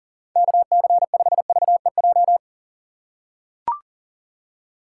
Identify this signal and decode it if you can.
Morse code, heard as a keyed tone, about 40 wpm, 700 Hz. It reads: KC5VEJ
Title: USA Call Signs - 6 Characters - 2x Character Spacing - 40wpm